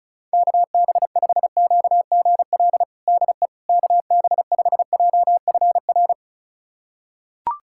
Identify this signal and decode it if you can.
Morse code, heard as a keyed tone, about 34 wpm, 690 Hz. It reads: KB5QGL DE KB5JFR
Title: USA Callsign DE Callsign - 6 Characters - 35wpm